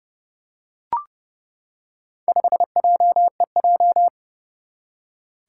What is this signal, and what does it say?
Morse code, heard as a keyed tone, about 31 wpm, 700 Hz. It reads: 5JEJ
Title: Four Challenging Letter-Number Combinations 30wpm